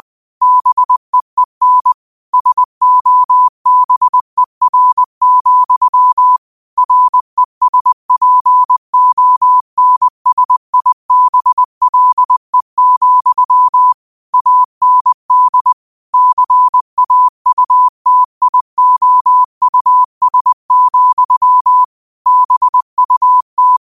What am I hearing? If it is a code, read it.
Morse code, heard as a keyed tone, about 20 wpm, 1000 Hz. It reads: BEEN SOBER, RESPONSIBLE, AND CAUTIOUS, BUT
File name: qotdfiles / 20wpm